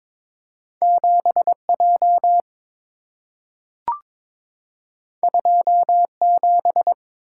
Morse code, heard as a keyed tone, about 22 wpm, 700 Hz. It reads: 7J 27
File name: practice / 22wpm / Two Letter-Number Combinations - 22wpm